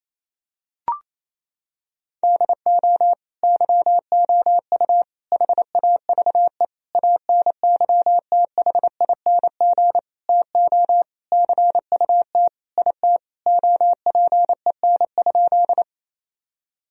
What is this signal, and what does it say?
Morse code, heard as a keyed tone, about 28 wpm, 700 Hz. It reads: DO YOU HAVE ANYTHING TO CUT IT OPEN?
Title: Sentences from Top 1000 Words - Review Part 1 - 28wpm